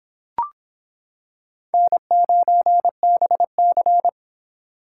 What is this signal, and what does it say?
Morse code, heard as a keyed tone, about 26 wpm, 700 Hz. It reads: N9BC